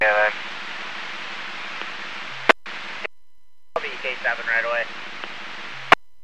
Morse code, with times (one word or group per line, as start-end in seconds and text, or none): none